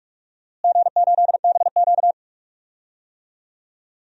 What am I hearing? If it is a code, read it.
Morse code, heard as a keyed tone, about 46 wpm, 700 Hz. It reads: G8BQ
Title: DXCC-100 Call Signs - 4 Characters - Repeated 2x - 45wpm